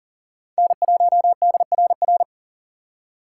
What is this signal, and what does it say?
Morse code, heard as a keyed tone, about 39 wpm, 700 Hz. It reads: N1DRR
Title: USA Call Signs - 3x Character Spacing - 40wpm